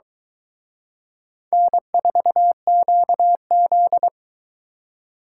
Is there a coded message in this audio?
Yes: Morse code from 1.5 to 4.1 s, heard as a keyed tone, about 23 wpm, 700 Hz. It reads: N4QZ